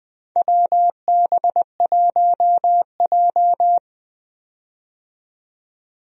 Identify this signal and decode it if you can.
Morse code, heard as a keyed tone, about 20 wpm, 700 Hz. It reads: WB1J
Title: USA Call Signs - 4 Characters - 20wpm